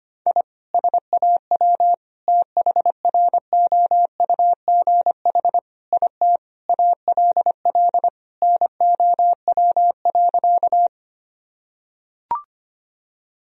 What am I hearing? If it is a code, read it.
Morse code, heard as a keyed tone, about 25 wpm, 700 Hz. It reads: I SAW THROUGH IT ALL NOW.
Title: Sentences from Top 1000 Words - Review Part 1 - 25wpm